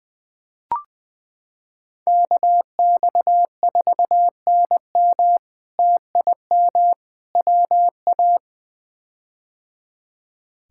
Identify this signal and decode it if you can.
Morse code, heard as a keyed tone, about 20 wpm, 700 Hz. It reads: KX4NM TIM WA
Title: K1USN SST QSO Practice - 20wpm